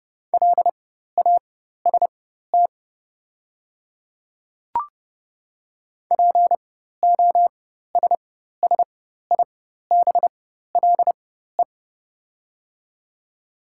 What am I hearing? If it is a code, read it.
Morse code, heard as a keyed tone, about 30 wpm, spaced as for 14 wpm, 690 Hz. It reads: LAST POSSIBLE
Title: 4x Character Spacing - Top 500 Words - 30wpm